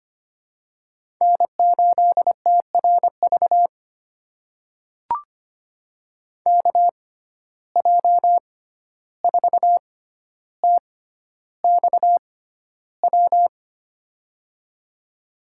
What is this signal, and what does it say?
Morse code, heard as a keyed tone, about 25 wpm, spaced as for 12 wpm, 700 Hz. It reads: N8TRV KJ4TXW
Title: USA Call Signs - 6x Character Spacing - 25wpm